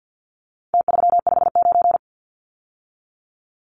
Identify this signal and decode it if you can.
Morse code, heard as a keyed tone, about 50 wpm, 700 Hz. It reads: T359
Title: Numbers - Contest Style - 4-Digit 50wpm